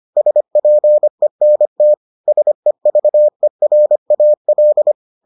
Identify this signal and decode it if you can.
Morse code, heard as a keyed tone, about 25 wpm, 600 Hz. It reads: SPENT SEVERAL